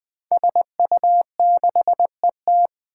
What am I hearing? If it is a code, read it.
Morse code, heard as a keyed tone, about 20 wpm, 700 Hz. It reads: SU6ET
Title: Five Challenging Letter-Number Combinations 20wpm